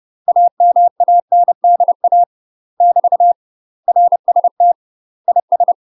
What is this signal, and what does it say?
Morse code, heard as a keyed tone, about 30 wpm, 700 Hz. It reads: AMANDA <BT> RST IS
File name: UK MASTER QSOs TRAINING/30WPM